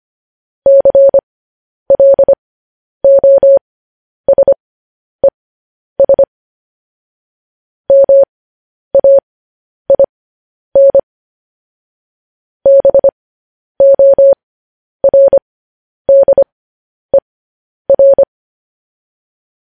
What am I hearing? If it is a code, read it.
Morse code, heard as a keyed tone, about 25 wpm, spaced as for 10 wpm, 550 Hz. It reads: CLOSES MAIN BORDER
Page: news headlines in morse code at 10 wpm